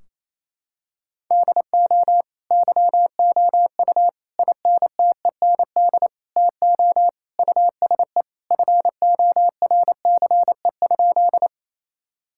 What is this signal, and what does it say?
Morse code, heard as a keyed tone, about 28 wpm, 700 Hz. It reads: DO YOU INTEND TO USE FORCE?
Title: Sentences from Top 1600 Words 28wpm